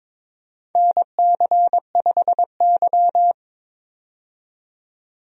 Morse code, heard as a keyed tone, about 22 wpm, 700 Hz. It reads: NC5Y
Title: USA Call Signs - 4 Characters - 22wpm